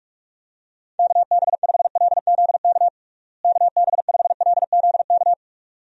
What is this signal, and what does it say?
Morse code, heard as a keyed tone, about 41 wpm, 700 Hz. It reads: KB5LZK KB5LZK
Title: USA Call Signs - 6 Characters - Repeated 2x - 45wpm